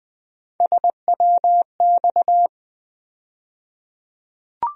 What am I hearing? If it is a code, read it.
Morse code, heard as a keyed tone, about 20 wpm, 700 Hz. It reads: SWX
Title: USA Call Sign Postfixes - 3 Letters 20wpm